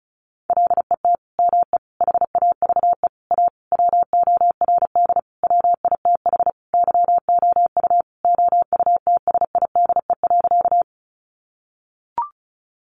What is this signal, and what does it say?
Morse code, heard as a keyed tone, about 34 wpm, 700 Hz. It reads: LET ME HAVE A WORD WITH YOU OUTSIDE.
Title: Sentences from Top 900 Words 35wpm